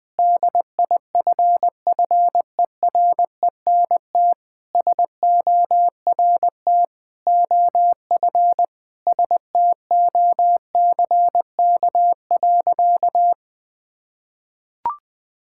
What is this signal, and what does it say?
Morse code, heard as a keyed tone, about 20 wpm, 700 Hz. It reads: DIFFERENT SORT OF STOCK.
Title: Sentences from Top 1000 Words - Review Part 5 - 20wpm